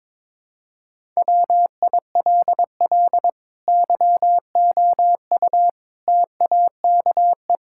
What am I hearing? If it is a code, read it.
Morse code, heard as a keyed tone, about 22 wpm, 700 Hz. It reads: WILL YOU TAKE
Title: Sentences from Top 1000 Words - Review Part 6 - 22wpm